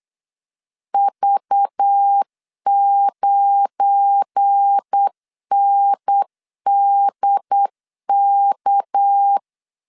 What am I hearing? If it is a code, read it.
Morse code, heard as a keyed tone, about 8 wpm, 800 Hz. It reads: V9NDK
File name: LEZIONI / 7 GF9Z / 7 GF9 8wpm 3545